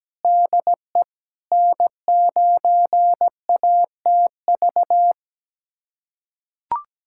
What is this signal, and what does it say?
Morse code, heard as a keyed tone, about 17 wpm, 700 Hz. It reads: DE N9ATV
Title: USA Callsign DE Callsign - 5 Characters - 17wpm